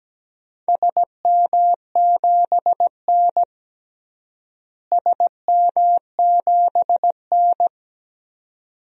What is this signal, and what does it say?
Morse code, heard as a keyed tone, about 17 wpm, 700 Hz. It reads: SM7N SM7N
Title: DXCC-100 Call Signs - 4 Characters - Repeated 2x - 17wpm